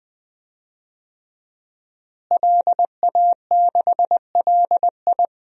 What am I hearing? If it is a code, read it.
Morse code, heard as a keyed tone, about 20 wpm, 700 Hz. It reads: LA6LI
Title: DXCC-100 Call Signs - 5 Characters - Repeated 2x - 20wpm